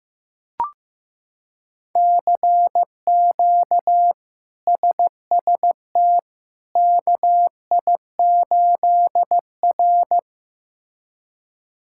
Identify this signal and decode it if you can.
Morse code, heard as a keyed tone, about 15 wpm, 700 Hz. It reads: CQ SST KI8R
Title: K1USN SST QSO Practice - 15wpm